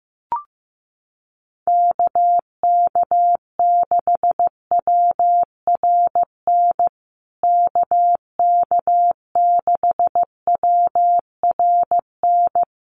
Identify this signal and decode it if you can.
Morse code, heard as a keyed tone, about 15 wpm, 700 Hz. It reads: KK6WRN KK6WRN
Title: USA Call Signs Repeated 2x 15wpm